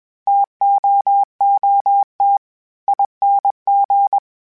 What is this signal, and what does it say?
Morse code, heard as a keyed tone, about 22 wpm, spaced as for 19 wpm, 800 Hz. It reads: TOOTING
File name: Misc / Misc-mp3s / TO @20